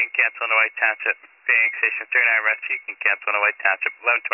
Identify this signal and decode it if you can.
Emergency Alert System (SAME) tones, heard as harsh buzzing bursts, with speech. It is heard from 0.8 to 4.0 s.